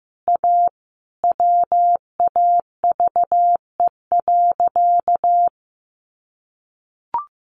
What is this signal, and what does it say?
Morse code, heard as a keyed tone, about 15 wpm, 700 Hz. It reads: A WAVE.
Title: Sentences from Top 1400 Words 15wpm